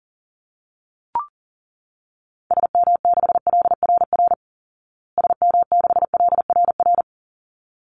Morse code, heard as a keyed tone, about 40 wpm, 700 Hz. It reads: SM6LRR SM6LRR
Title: Call Signs Repeated 3x - 40wpm